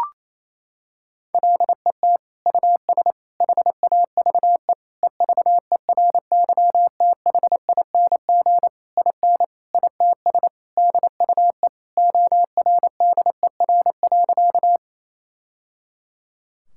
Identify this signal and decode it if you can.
Morse code, heard as a keyed tone, about 28 wpm, 700 Hz. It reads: LET US HAVE EVERYTHING IN ITS DUE ORDER.
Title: Sentences from Top 1000 Words - Review Part 6 - 28wpm